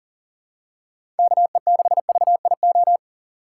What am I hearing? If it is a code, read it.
Morse code, heard as a keyed tone, about 37 wpm, 700 Hz. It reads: KE6VIO
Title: USA Call Signs - 6 Characters - 2x Character Spacing - 40wpm